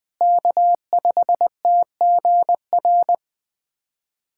Morse code, heard as a keyed tone, about 20 wpm, 700 Hz. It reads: K5TGR
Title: USA Callsign DE Callsign - 5 Characters - 20wpm